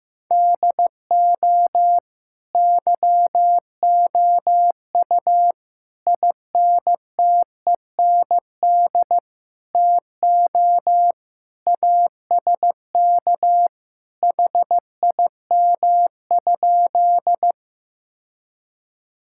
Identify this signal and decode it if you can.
Morse code, heard as a keyed tone, about 15 wpm, 700 Hz. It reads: DO YOU INTEND TO ASK HIM?
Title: Sentences from Top 1600 Words 15wpm